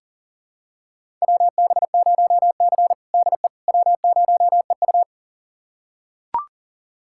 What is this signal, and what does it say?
Morse code, heard as a keyed tone, about 40 wpm, 700 Hz. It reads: WB0C DE W0EU
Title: USA Callsign DE Callsign - 4 Characters - 40wpm